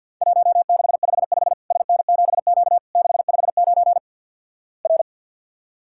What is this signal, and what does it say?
Morse code, heard as a keyed tone, about 50 wpm, 700 Hz. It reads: 1655 SN7Q 659 R
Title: Worked All Europe DX Contest - QTC Exchange - 50wpm